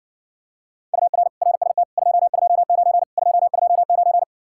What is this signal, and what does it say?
Morse code, heard as a keyed tone, about 66 wpm, 700 Hz. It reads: UR RST 229 229